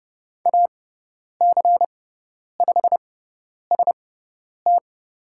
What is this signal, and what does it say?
Morse code, heard as a keyed tone, about 30 wpm, spaced as for 10 wpm, 700 Hz. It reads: AC5ST
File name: Lesson 17g - Letter u - Call Signs 30wpm